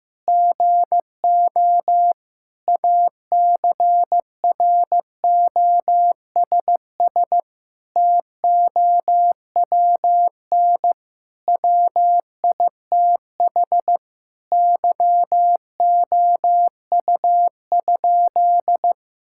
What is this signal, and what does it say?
Morse code, heard as a keyed tone, about 15 wpm, 700 Hz. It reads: GO ACROSS TOWN WITH YOU?
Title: Sentences from Top 1000 Words - Review Part 5 - 15wpm